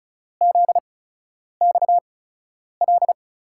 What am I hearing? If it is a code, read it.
Morse code, heard as a keyed tone, about 36 wpm, spaced as for 10 wpm, 700 Hz. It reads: ZXL
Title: USA Call Signs - 6 Characters - 8x Character Spacing - 35wpm